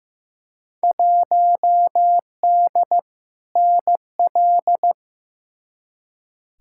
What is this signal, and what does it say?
Morse code, heard as a keyed tone, about 15 wpm, 700 Hz. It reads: 1D NL
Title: Field Day Exchanges Encore - 15wpm